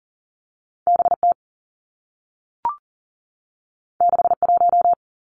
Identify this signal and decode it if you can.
Morse code, heard as a keyed tone, about 40 wpm, 700 Hz. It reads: B T 6 1
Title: Two Challenging Letter-Number Combinations 40wpm